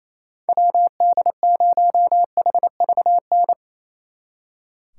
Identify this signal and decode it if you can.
Morse code, heard as a keyed tone, about 34 wpm, 700 Hz. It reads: WD0HVN
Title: Speed Racing - USA Call Signs 28wpm